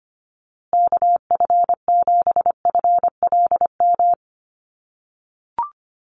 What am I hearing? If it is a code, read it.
Morse code, heard as a keyed tone, about 25 wpm, 700 Hz. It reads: KF7FLM
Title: USA Call Signs Repeated 3x 25wpm